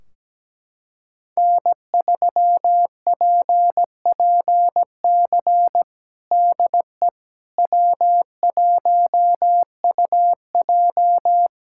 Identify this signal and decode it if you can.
Morse code, heard as a keyed tone, about 17 wpm, 700 Hz. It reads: N3PPC DE W1UJ
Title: Practice QSO - Exchange 4 - Ending - 17wpm